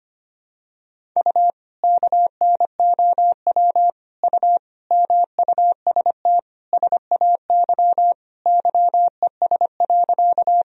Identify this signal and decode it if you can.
Morse code, heard as a keyed tone, about 25 wpm, 700 Hz. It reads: U KNOW U MUST SAY YES.
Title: Sentences from Top 500 Words and Abbreviations 25wpm